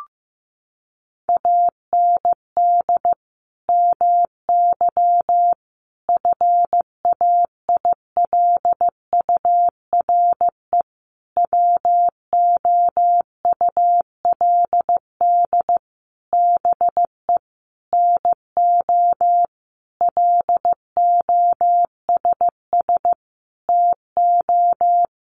Morse code, heard as a keyed tone, about 15 wpm, 700 Hz. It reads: AND MY FAILURE WOULD BE NO LOSS TO
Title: Sentences from Top 1400 Words 15wpm